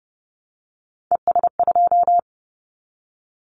Morse code, heard as a keyed tone, about 30 wpm, 700 Hz. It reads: ES2
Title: Callsign Prefixes for Top 50 DXCC - 30wpm